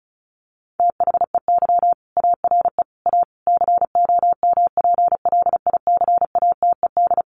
Morse code, heard as a keyed tone, about 34 wpm, 700 Hz. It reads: THEY ARE A COMPLICATED
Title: Sentences from Top 1200 Words 35wpm